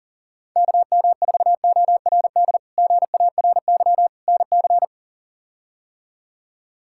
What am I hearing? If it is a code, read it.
Morse code, heard as a keyed tone, about 40 wpm, 700 Hz. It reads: KM4ORD GARY NC
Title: K1USN SST QSO Practice - 40wpm